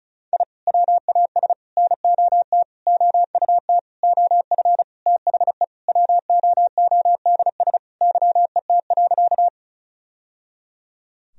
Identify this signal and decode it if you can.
Morse code, heard as a keyed tone, about 36 wpm, 700 Hz. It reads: I WAS NOT OUT OF THE WOODS YET.
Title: Sentences from Top 1000 Words - Review Part 2 - 35wpm